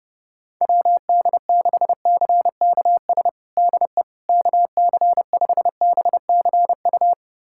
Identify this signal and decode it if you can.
Morse code, heard as a keyed tone, about 31 wpm, 700 Hz. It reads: WD6CKS DE KC5BCU
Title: USA Callsign DE Callsign - 6 Characters - 30wpm